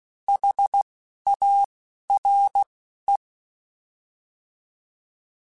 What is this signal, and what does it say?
Morse code, heard as a keyed tone, about 16 wpm, 800 Hz. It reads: H A R E